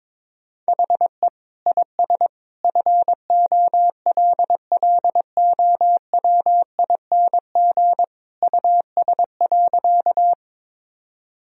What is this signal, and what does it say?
Morse code, heard as a keyed tone, about 22 wpm, 700 Hz. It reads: HE IS FOLLOWING US.
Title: Sentences from Top 1000 Words - Review Part 4 - 22wpm